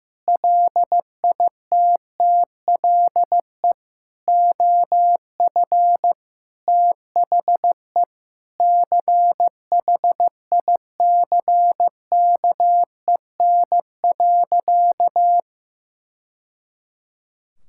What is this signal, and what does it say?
Morse code, heard as a keyed tone, about 16 wpm, 700 Hz. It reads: LITTLE OF THE CHICKEN.
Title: Sentences from Top 1000 Words - Review Part 5 - 15wpm